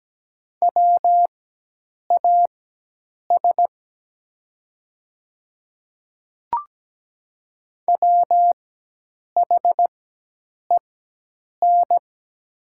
Morse code, heard as a keyed tone, about 17 wpm, spaced as for 8 wpm, 700 Hz. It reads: WAS WHEN
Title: 4x Character Spacing - Top 100 Words - 17wpm